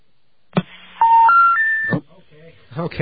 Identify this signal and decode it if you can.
Special information tone, with speech.